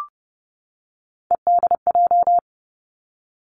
Morse code, heard as a keyed tone, about 30 wpm, 700 Hz. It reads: EDJ
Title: Three Challenging Letter-Number Combinations 30wpm